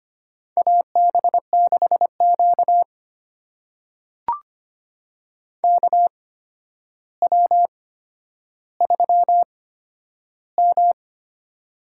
Morse code, heard as a keyed tone, about 25 wpm, spaced as for 11 wpm, 700 Hz. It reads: AB6Q KW3M
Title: USA Call Signs - 4 Characters - 8x Character Spacing - 25wpm